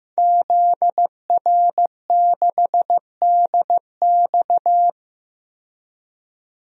Morse code, heard as a keyed tone, about 15 wpm, 700 Hz. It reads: ZR6DX
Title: DXCC-100 Call Signs - 5 Characters - 15wpm